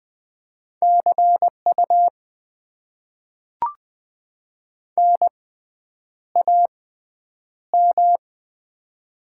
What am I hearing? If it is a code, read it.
Morse code, heard as a keyed tone, about 20 wpm, spaced as for 9 wpm, 700 Hz. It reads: CU NAM
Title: Top 90 QSO Elements - 6x Character Spacing - 20wpm